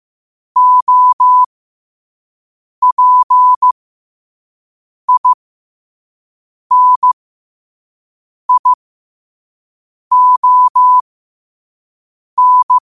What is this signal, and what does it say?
Morse code, heard as a keyed tone, about 15 wpm, spaced as for 5 wpm, 1000 Hz. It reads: OPINION
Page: Quotes for Wed, 13 Aug 2025 in Morse Code at 5 words per minute.